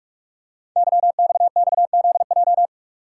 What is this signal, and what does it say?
Morse code, heard as a keyed tone, about 44 wpm, 700 Hz. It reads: YXXZJ
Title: Code Groups - 5 Letters - 45wpm